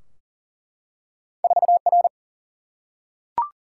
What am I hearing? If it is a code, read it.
Morse code, heard as a keyed tone, about 40 wpm, 700 Hz. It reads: 4 R
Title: Single-Letter Single-Number 40wpm